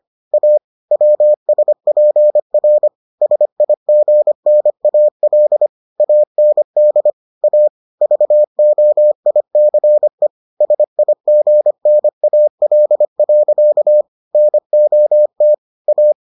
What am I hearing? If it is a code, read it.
Morse code, heard as a keyed tone, about 25 wpm, 600 Hz. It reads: A WSPR SIGNAL AND A VOICE SIGNAL. NOT A